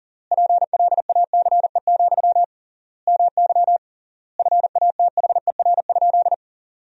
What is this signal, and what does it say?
Morse code, heard as a keyed tone, about 40 wpm, 700 Hz. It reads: PLACE, MY FATHER?
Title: 3x Word-Spacing - Sentences from Top 400 Words - 40wpm